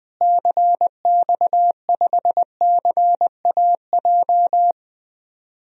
Morse code, heard as a keyed tone, about 21 wpm, 700 Hz. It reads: CX5CAJ
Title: DXCC-100 Call Signs - 6 Characters - Repeated 2x - 20wpm